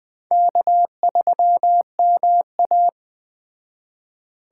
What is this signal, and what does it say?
Morse code, heard as a keyed tone, about 20 wpm, 700 Hz. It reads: K3MA
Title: Speed Racing - USA Call Signs 17wpm